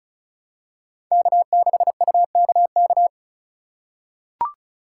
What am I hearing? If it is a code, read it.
Morse code, heard as a keyed tone, about 36 wpm, 700 Hz. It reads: K6UKK
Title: USA Call Signs - 6x Character Spacing - 35wpm